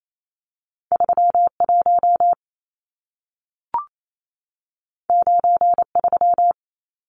Morse code, heard as a keyed tone, about 28 wpm, 700 Hz. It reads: 31 93
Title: Numbers - 2-Digit 28wpm